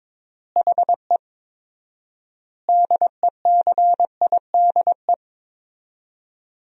Morse code, heard as a keyed tone, about 22 wpm, 700 Hz. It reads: HE DECIDE
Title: 4x Word-Spacing - Sentences from Top 500 Words - 22wpm